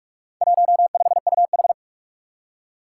Morse code, heard as a keyed tone, about 42 wpm, 700 Hz. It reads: 15UH
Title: Four Challenging Letter-Number Combinations 45wpm